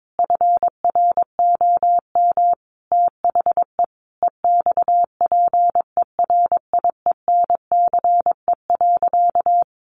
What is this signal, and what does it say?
Morse code, heard as a keyed tone, about 22 wpm, 700 Hz. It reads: FROM THE EXPERIENCE.
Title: Sentences from Top 1300 Words 22wpm